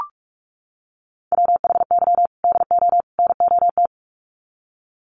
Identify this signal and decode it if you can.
Morse code, heard as a keyed tone, about 44 wpm, 700 Hz. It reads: WHY DO NOT
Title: Sets of 3 Words 45wpm